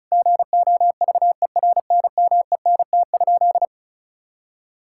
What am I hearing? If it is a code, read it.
Morse code, heard as a keyed tone, about 36 wpm, 700 Hz. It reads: GOVERNMENT?
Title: Sentences from Top 1000 Words - Review Part 5 - 35wpm